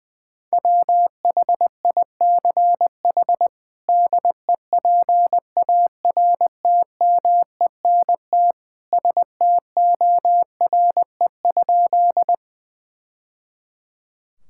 Morse code, heard as a keyed tone, about 21 wpm, 700 Hz. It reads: WHICH DEPARTMENT STORE?
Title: Sentences from Top 1000 Words - Review Part 5 - 20wpm